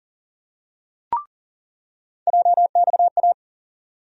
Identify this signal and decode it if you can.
Morse code, heard as a keyed tone, about 40 wpm, 700 Hz. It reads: JXA
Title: USA Call Sign Postfixes - 3 Letters 40wpm